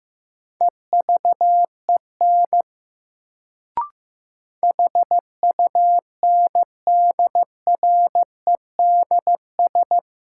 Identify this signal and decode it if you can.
Morse code, heard as a keyed tone, about 15 wpm, 700 Hz. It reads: EVEN HUNDREDS
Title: Sets of 4 Words - New Word Drill - 15wpm